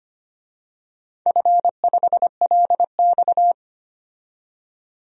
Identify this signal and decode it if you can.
Morse code, heard as a keyed tone, about 25 wpm, 700 Hz. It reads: F5LX
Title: DXCC-100 Call Signs - 4 Characters - 25wpm